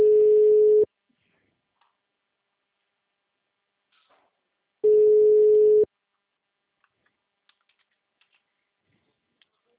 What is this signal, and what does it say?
Ringback tone (Europe, 425 Hz).
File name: out- -2025- - - .185